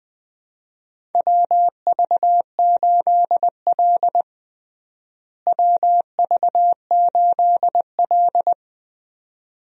Morse code, heard as a keyed tone, about 20 wpm, 700 Hz. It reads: WV8L WV8L
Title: USA Call Signs - 4 Characters - Repeated 2x - 20wpm